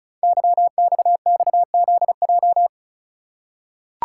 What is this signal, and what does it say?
Morse code, heard as a keyed tone, about 36 wpm, 700 Hz. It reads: YXXZJ
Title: Code Groups - 5 Letters - 35wpm